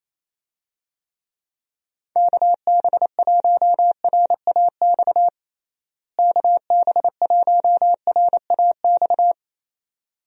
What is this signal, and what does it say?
Morse code, heard as a keyed tone, about 28 wpm, 700 Hz. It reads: KB1RAX KB1RAX
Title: USA Call Signs - 6 Characters - Repeated 2x - 28wpm